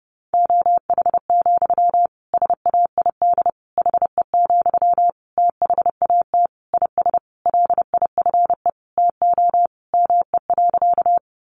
Morse code, heard as a keyed tone, about 30 wpm, 700 Hz. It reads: OH, SAID HE, THAT IS LIFE TO ME.
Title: Sentences from Top 500 Words 30wpm